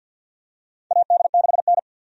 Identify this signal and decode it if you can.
Morse code, heard as a keyed tone, about 49 wpm, 700 Hz. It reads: AD6N
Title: USA Call Signs - 4 Characters - 50wpm